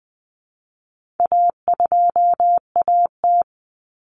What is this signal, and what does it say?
Morse code, heard as a keyed tone, about 20 wpm, 700 Hz. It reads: A2AT
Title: Common Cut Numbers - 4-Digit 20wpm